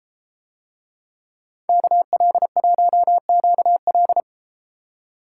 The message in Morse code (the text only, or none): KL1QL